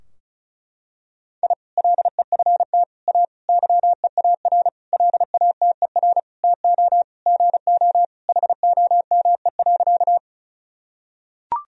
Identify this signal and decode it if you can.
Morse code, heard as a keyed tone, about 34 wpm, 700 Hz. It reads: I LEFT A YEAR LATER TO GO HOME.
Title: Sentences from Top 1000 Words - Review Part 4 - 35wpm